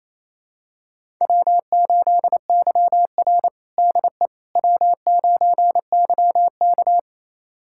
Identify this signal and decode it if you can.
Morse code, heard as a keyed tone, about 28 wpm, 700 Hz. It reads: W8YR DE W9YK
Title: USA Callsign DE Callsign - 4 Characters - 28wpm